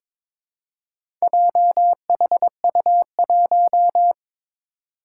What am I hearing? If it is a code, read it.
Morse code, heard as a keyed tone, about 22 wpm, 700 Hz. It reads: JHU1
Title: Four Challenging Letter-Number Combinations 22wpm